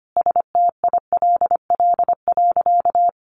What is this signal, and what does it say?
Morse code, heard as a keyed tone, about 25 wpm, 700 Hz. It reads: STILL.
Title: Sentences from Top 1000 Words 25wpm